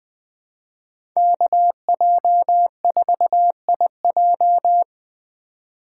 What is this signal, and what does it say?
Morse code, heard as a keyed tone, about 20 wpm, 700 Hz. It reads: KJ4IJ
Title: USA Call Signs - 5 Characters - 8x Character Spacing - 20wpm